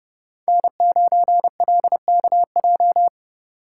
Morse code, heard as a keyed tone, about 28 wpm, 700 Hz. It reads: N9LKJ